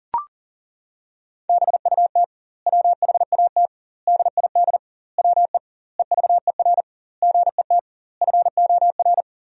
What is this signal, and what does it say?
Morse code, heard as a keyed tone, about 40 wpm, 700 Hz. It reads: BUT WHAT DID WE EVER GET FOR
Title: 2x Word-Spacing - Sentences from Top 400 Words - 40wpm